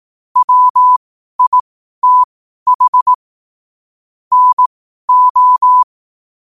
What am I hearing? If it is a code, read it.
Morse code, heard as a keyed tone, about 18 wpm, spaced as for 12 wpm, 1000 Hz. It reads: WITH NO
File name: qotdfiles / 12wpm